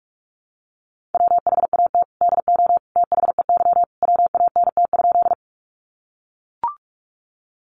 Morse code, heard as a keyed tone, about 44 wpm, 700 Hz. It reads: WHAT DO THEY WANT?
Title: Sentences from Top 200 Words 45wpm